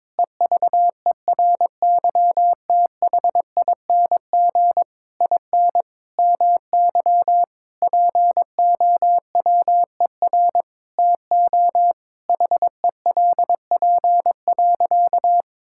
Morse code, heard as a keyed tone, about 22 wpm, 700 Hz. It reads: EVERYTHING IN MY POWER TO HELP.
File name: practice / 22wpm / Sentences with 8-Word Phrases - 22wpm